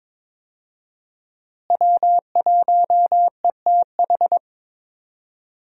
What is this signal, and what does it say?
Morse code, heard as a keyed tone, about 23 wpm, 700 Hz. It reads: W1ETH
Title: USA Call Signs - 5 Characters - Repeated 2x - 22wpm